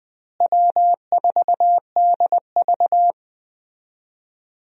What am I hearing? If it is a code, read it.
Morse code, heard as a keyed tone, about 20 wpm, 700 Hz. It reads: W4DV
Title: USA Call Signs - 4 Characters - Repeated 2x - 20wpm